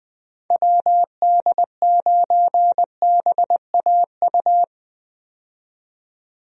Morse code, heard as a keyed tone, about 20 wpm, 700 Hz. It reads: WD9BAU
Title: USA Call Signs - 6 Characters - 20wpm